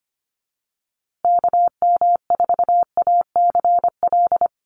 Morse code, heard as a keyed tone, about 25 wpm, 700 Hz. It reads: KM4ACL
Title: USA Call Signs Repeated 3x 25wpm